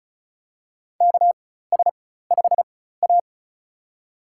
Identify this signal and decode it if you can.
Morse code, heard as a keyed tone, about 36 wpm, spaced as for 16 wpm, 700 Hz. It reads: KS5A
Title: USA Call Signs - 4 Characters - 4x Character Spacing - 35wpm